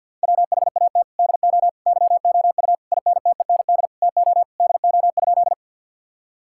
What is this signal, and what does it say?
Morse code, heard as a keyed tone, about 50 wpm, 700 Hz. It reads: WHAT DO YOU INTEND TO DO?
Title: Sentences from Top 1600 Words 50wpm